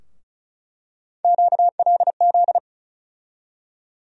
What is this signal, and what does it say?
Morse code, heard as a keyed tone, about 36 wpm, 700 Hz. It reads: QLZ
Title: USA Call Sign Postfixes - 3 Letters 35wpm